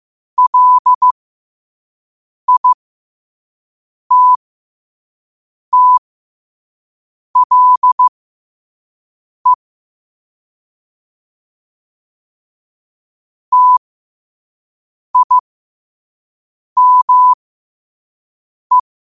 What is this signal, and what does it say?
Morse code, heard as a keyed tone, about 15 wpm, spaced as for 5 wpm, 1000 Hz. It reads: LITTLE TIME
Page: Quotes for Tue, 12 Aug 2025 in Morse Code at 5 words per minute.